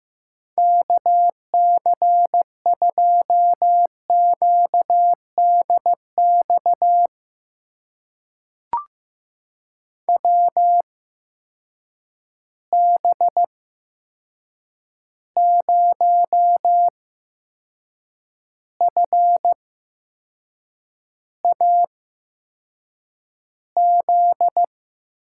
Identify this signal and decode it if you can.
Morse code, heard as a keyed tone, about 15 wpm, spaced as for 6 wpm, 700 Hz. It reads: KC2QDX WB0FAZ
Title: USA Call Signs - 6 Characters - 8x Character Spacing - 15wpm